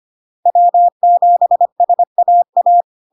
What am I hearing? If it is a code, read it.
Morse code, heard as a keyed tone, about 25 wpm, 700 Hz. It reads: W7SAA
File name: LARK-FD-2022-Log-in-CW-25wpm-700-Hz